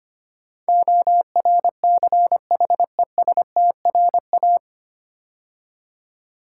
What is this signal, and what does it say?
Morse code, heard as a keyed tone, about 25 wpm, 700 Hz. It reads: ORCHESTRA
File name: practice / 25wpm / 9 Letter Words - First 3 Letters Given - 25wpm